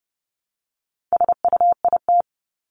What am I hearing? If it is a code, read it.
Morse code, heard as a keyed tone, about 30 wpm, 700 Hz. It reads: SUIT
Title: Top 1501-1750 Words 30wpm